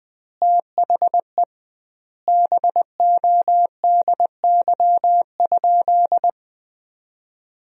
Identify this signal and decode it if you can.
Morse code, heard as a keyed tone, about 20 wpm, 700 Hz. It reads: THE BODY?
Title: 2x Word-Spacing - Sentences from Top 400 Words - 20wpm